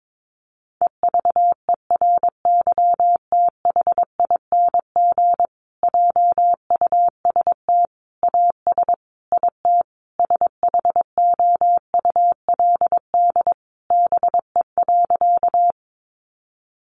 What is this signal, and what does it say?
Morse code, heard as a keyed tone, about 22 wpm, 700 Hz. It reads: EVERYTHING JUST AS IT SHOULD BE.
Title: Sentences from Top 700 Words 22wpm